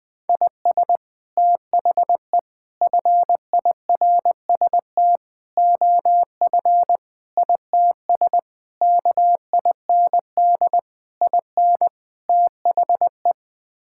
Morse code, heard as a keyed tone, about 20 wpm, 700 Hz. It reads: IS THE FIRST OF ITS KIND IN THE
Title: Sets of 8 Words - 20wpm